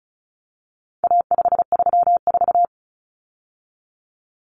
Morse code, heard as a keyed tone, about 34 wpm, 700 Hz. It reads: A534
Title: Common Cut Numbers - 4-Digit 35wpm